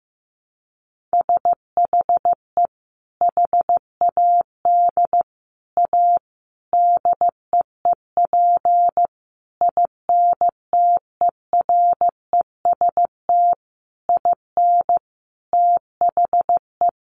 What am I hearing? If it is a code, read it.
Morse code, heard as a keyed tone, about 15 wpm, 700 Hz. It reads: SHE HAD A DEEP INTEREST IN THE
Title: Sentences from Top 1300 Words 15wpm